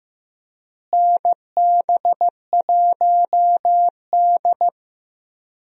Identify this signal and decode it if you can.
Morse code, heard as a keyed tone, about 16 wpm, 700 Hz. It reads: NB1D
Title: USA Call Signs - 4 Characters - Repeated 3x - 15wpm